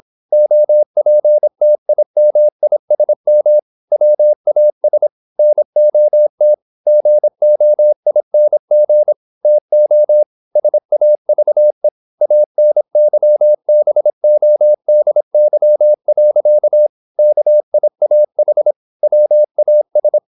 Morse code, heard as a keyed tone, about 26 wpm, 600 Hz. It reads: OPTIMISM WAS NOT GOING TO SAVE ANYBODY. KIAH WAS
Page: Never 26wpm | CW med Gnister